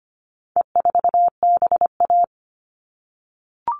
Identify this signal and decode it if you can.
Morse code, heard as a keyed tone, about 25 wpm, 700 Hz. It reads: E4BA
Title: All Cut Numbers - 4-Digit 25wpm